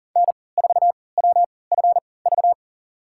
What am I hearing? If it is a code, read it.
Morse code, heard as a keyed tone, about 37 wpm, spaced as for 22 wpm, 700 Hz. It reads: N4WFV
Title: USA Call Signs - 5 Characters - 3x Character Spacing - 40wpm